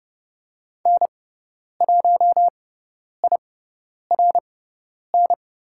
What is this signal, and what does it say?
Morse code, heard as a keyed tone, about 30 wpm, spaced as for 10 wpm, 700 Hz. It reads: N1IRN
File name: Lesson 18g - Letter m - Call Signs 30wpm